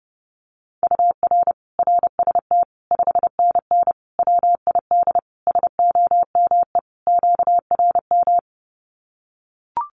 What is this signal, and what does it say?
Morse code, heard as a keyed tone, about 30 wpm, 700 Hz. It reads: UR RST 5NN WID SOME QRM
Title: Common QSO Phrases 30wpm